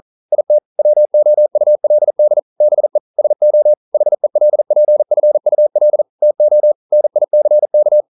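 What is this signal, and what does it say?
Morse code, heard as a keyed tone, about 42 wpm, 600 Hz. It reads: IT WOULD BE SO HELPFUL TO NICK
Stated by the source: Morse code at 40 wpm, practice text, CW med Gnister.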